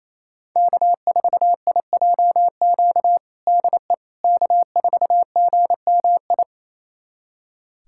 Morse code, heard as a keyed tone, about 28 wpm, 700 Hz. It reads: K4IJQ DE K4GMI
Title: USA Callsign DE Callsign 28wpm